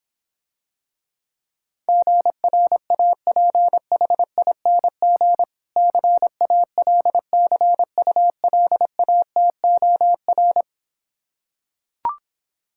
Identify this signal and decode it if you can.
Morse code, heard as a keyed tone, about 26 wpm, 700 Hz. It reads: GRAPHING CALCULATOR